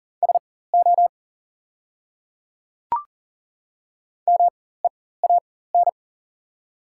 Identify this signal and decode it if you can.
Morse code, heard as a keyed tone, about 40 wpm, spaced as for 19 wpm, 700 Hz. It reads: SO MEAN
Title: 4x Character Spacing - Top 200 Words - 40wpm